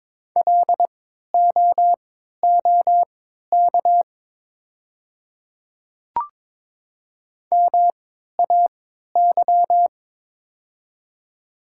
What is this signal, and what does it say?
Morse code, heard as a keyed tone, about 22 wpm, spaced as for 13 wpm, 700 Hz. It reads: LOOK MAY
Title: 3x Character Spacing - Top 100 Words - 22wpm